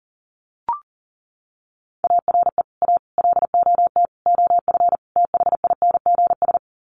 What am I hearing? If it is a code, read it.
Morse code, heard as a keyed tone, about 40 wpm, 700 Hz. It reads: ARE A LOT OF THINGS
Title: Sets of 5 Words Encore2 40wpm